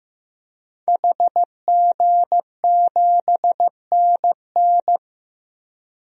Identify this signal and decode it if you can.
Morse code, heard as a keyed tone, about 15 wpm, 700 Hz. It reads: HG7NN
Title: DXCC-100 Call Signs - 5 Characters - 15wpm